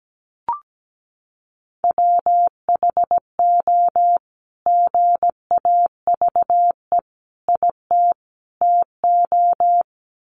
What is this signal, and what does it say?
Morse code, heard as a keyed tone, about 17 wpm, 700 Hz. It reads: WHO GAVE IT TO
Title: Sentences from Top 500 Words 17wpm